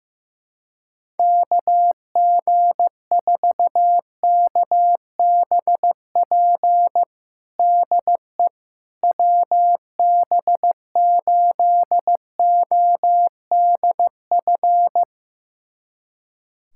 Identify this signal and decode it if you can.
Morse code, heard as a keyed tone, about 15 wpm, 700 Hz. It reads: KG4KBP DE WB8ODF
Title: USA Callsign DE Callsign 15wpm